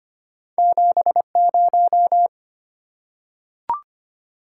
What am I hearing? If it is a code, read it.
Morse code, heard as a keyed tone, about 25 wpm, 700 Hz. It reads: 70
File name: Two Letter-Number Combinations - 25wpm